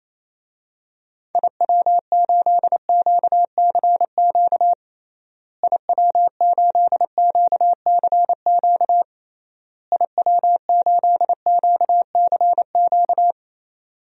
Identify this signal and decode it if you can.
Morse code, heard as a keyed tone, about 28 wpm, 700 Hz. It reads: IW8QCQ IW8QCQ IW8QCQ
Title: DXCC-100 Call Signs - 6 Characters - Repeated 3x - 28wpm